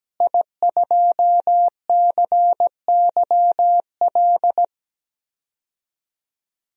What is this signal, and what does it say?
Morse code, heard as a keyed tone, about 17 wpm, 700 Hz. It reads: I2CYL
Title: DXCC-100 Call Signs - 5 Characters - Repeated 3x - 17wpm